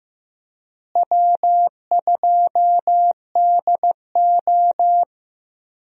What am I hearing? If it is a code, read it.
Morse code, heard as a keyed tone, about 15 wpm, 700 Hz. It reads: W2DO
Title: USA Call Signs - 4 Characters - Repeated 3x - 15wpm